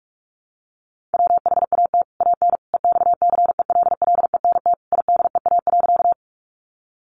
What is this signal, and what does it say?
Morse code, heard as a keyed tone, about 44 wpm, 700 Hz. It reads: WHAT AN EXCELLENT IDEA.
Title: Sentences from Top 1000 Words 45wpm